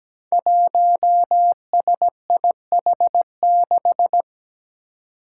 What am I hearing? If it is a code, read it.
Morse code, heard as a keyed tone, about 17 wpm, 700 Hz. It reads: 1SIH6
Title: Five Challenging Letter-Number Combinations 17wpm